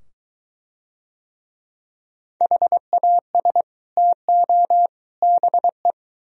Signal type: Morse code

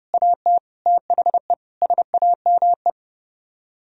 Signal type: Morse code

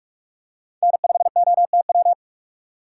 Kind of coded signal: Morse code

